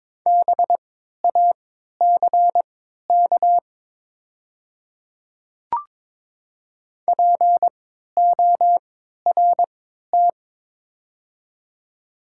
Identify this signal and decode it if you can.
Morse code, heard as a keyed tone, about 22 wpm, spaced as for 12 wpm, 700 Hz. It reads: BACK PORT
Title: 3x Character Spacing - Top 200 Words - 22wpm